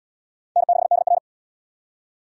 Morse code, heard as a keyed tone, about 77 wpm, 700 Hz. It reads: N6DR